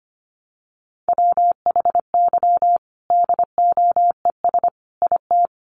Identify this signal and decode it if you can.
Morse code, heard as a keyed tone, about 25 wpm, 700 Hz. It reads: WHY DOES IT
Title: Sentences from Top 1300 Words 25wpm